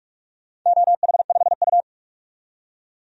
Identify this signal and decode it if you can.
Morse code, heard as a keyed tone, about 44 wpm, 700 Hz. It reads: OH5U
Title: DXCC-100 Call Signs - 4 Characters - 45wpm